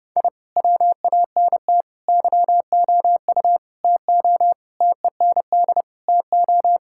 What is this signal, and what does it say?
Morse code, heard as a keyed tone, about 30 wpm, 700 Hz. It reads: I WANT YOU TO TEND TO
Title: Sentences from Top 1000 Words - Review Part 5 - 30wpm